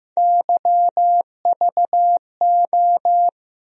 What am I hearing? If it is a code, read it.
Morse code, heard as a keyed tone, about 15 wpm, 700 Hz. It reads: YVO
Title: USA Call Sign Postfixes - 3 Letters 15wpm